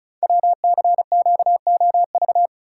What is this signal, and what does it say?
Morse code, heard as a keyed tone, about 36 wpm, 700 Hz. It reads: WCQOV
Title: Code Groups - 5 Letters - 35wpm